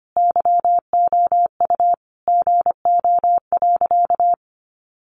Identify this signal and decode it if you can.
Morse code, heard as a keyed tone, about 25 wpm, 700 Hz. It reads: YOU GO.
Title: Sentences from Top 600 Words 25wpm